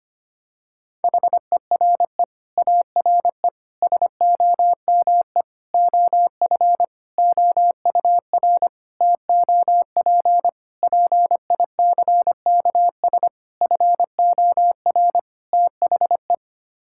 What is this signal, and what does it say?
Morse code, heard as a keyed tone, about 25 wpm, 700 Hz. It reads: HERE ARE SOME OF OUR TOP PICKS FOR THE
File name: Sets of 9 Words - 25wpm